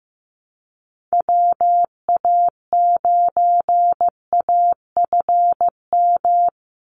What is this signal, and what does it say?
Morse code, heard as a keyed tone, about 15 wpm, 700 Hz. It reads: WA9AFM
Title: USA Call Signs 15wpm